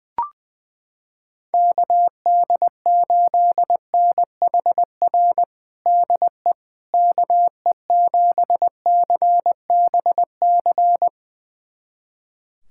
Morse code, heard as a keyed tone, about 21 wpm, 700 Hz. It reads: KD8NHR DE KE7CBC
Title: USA Callsign DE Callsign 20wpm